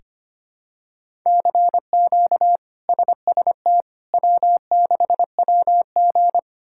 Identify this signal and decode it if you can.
Morse code, heard as a keyed tone, about 25 wpm, 700 Hz. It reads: CQ SST W6WG
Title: K1USN SST QSO Practice - 25wpm